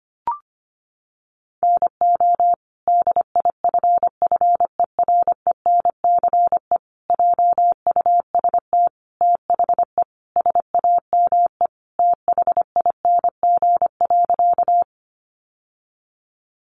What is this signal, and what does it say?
Morse code, heard as a keyed tone, about 25 wpm, 700 Hz. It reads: NO DIFFERENCE JUST THE SAME THING.
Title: Sentences from Top 800 Words 25wpm